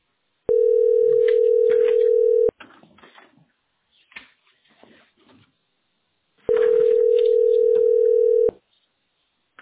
Ringback tone (North America), 440 + 480 Hz.